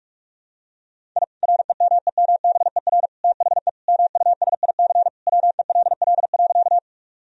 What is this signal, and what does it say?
Morse code, heard as a keyed tone, about 46 wpm, 700 Hz. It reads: I REMEMBER THE MUSIC WELL.
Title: Sentences from Top 1000 Words - Review Part 1 - 45wpm